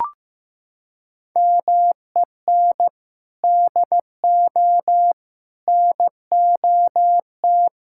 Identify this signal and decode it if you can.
Morse code, heard as a keyed tone, about 16 wpm, 700 Hz. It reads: MEN DO NOT
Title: Sentences from Top 1000 Words - Review Part 6 - 15wpm